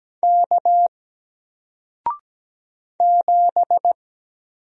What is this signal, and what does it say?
Morse code, heard as a keyed tone, about 17 wpm, 700 Hz. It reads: K 7
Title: Single Character (Letter:Number:Punctuation:Prosign) Alt 17wpm